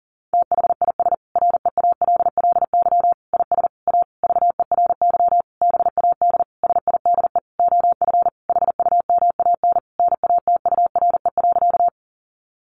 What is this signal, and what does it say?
Morse code, heard as a keyed tone, about 40 wpm, 700 Hz. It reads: THIS REALLY IS A VERY BAD SIDE OF HUMAN NATURE.
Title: Sentences from Top 700 Words 40wpm